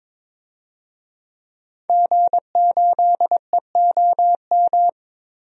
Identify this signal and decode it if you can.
Morse code, heard as a keyed tone, about 21 wpm, 700 Hz. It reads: G8EOM
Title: DXCC-100 Call Signs - 5 Characters - Repeated 2x - 22wpm